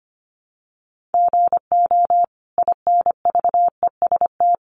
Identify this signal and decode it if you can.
Morse code, heard as a keyed tone, about 25 wpm, 700 Hz. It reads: GO INVEST
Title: Sentences from Top 1100 Words 25wpm